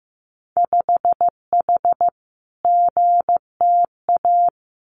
Morse code, heard as a keyed tone, about 15 wpm, 700 Hz. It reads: 5H GTA
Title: Winter Field Day Exchanges Encore 15wpm